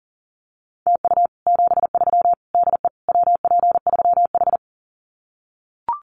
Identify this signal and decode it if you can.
Morse code, heard as a keyed tone, about 40 wpm, 700 Hz. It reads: TU 73 DE WP3H
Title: Common QSO Phrases Encore 40wpm